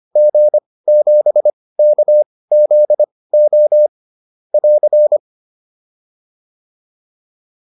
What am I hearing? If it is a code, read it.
Morse code, heard as a keyed tone, about 25 wpm, spaced as for 18 wpm, 600 Hz. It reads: G7KZO <AR>